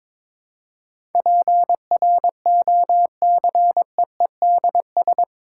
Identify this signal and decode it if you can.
Morse code, heard as a keyed tone, about 22 wpm, 700 Hz. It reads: PROCEEDS
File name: practice / 22wpm / Eight Letter Words - 22wpm